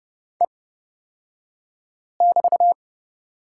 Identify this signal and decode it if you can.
Morse code, heard as a keyed tone, about 30 wpm, spaced as for 5 wpm, 700 Hz. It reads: E<BT>
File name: Lesson 37d - Letter x - Two Characters 30wpm